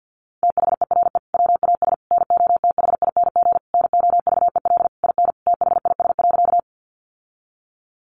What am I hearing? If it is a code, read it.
Morse code, heard as a keyed tone, about 50 wpm, 700 Hz. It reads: THERE WAS NOTHING NOVEL IN THIS.
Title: Sentences from Top 1300 Words 50wpm